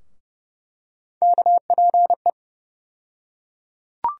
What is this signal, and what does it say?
Morse code, heard as a keyed tone, about 31 wpm, 700 Hz. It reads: KPE
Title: USA Call Sign Postfixes - 3 Letters 30wpm